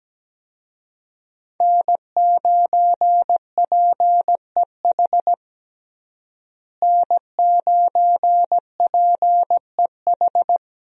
Morse code, heard as a keyed tone, about 17 wpm, 700 Hz. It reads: N9PEH N9PEH
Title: USA Call Signs - 5 Characters - Repeated 2x - 17wpm